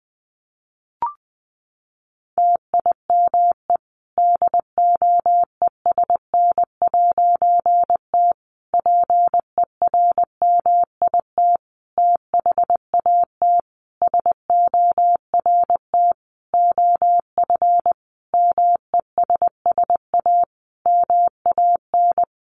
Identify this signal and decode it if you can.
Morse code, heard as a keyed tone, about 20 wpm, 700 Hz. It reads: TIME DOESN'T PERMIT THAT SORT OF MESS A MAN
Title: A Sound Of Thunder 20wpm